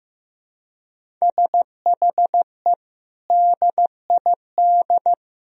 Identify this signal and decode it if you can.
Morse code, heard as a keyed tone, about 15 wpm, 700 Hz. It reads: SHE DID
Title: Sentences from Top 1000 Words - Review Part 1 - 15wpm